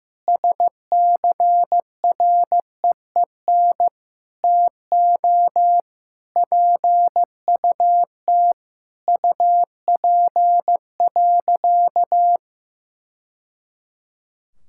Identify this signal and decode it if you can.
Morse code, heard as a keyed tone, about 15 wpm, 700 Hz. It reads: SCREEN TO PUT UP.
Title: Sentences from Top 1000 Words - Review Part 6 - 15wpm